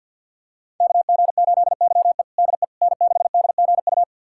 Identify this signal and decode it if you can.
Morse code, heard as a keyed tone, about 53 wpm, 700 Hz. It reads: KG8YE DE N6DGU
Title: USA Callsign DE Callsign - 5 Characters - 50wpm